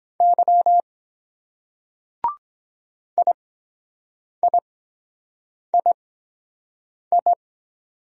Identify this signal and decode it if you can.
Morse code, heard as a keyed tone, about 23 wpm, spaced as for 7 wpm, 700 Hz. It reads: Y IIII